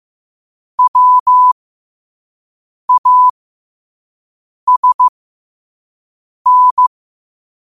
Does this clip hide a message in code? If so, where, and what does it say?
Yes: Morse code from 0.8 to 6.9 s, heard as a keyed tone, about 15 wpm, spaced as for 5 wpm, 1000 Hz. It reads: WASN